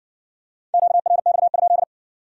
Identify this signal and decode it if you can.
Morse code, heard as a keyed tone, about 54 wpm, 700 Hz. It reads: QRX?